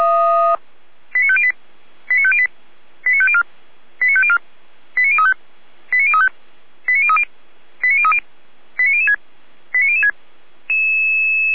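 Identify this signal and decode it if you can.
Five-tone selective call (ZVEI), heard as quick runs of five tones, tones of about 70 ms. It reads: 89508 89508 89504 89504 89035 89035 89030 89030 89006 89006
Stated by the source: ZVEI five-tone selective call 89099 received on an SDR.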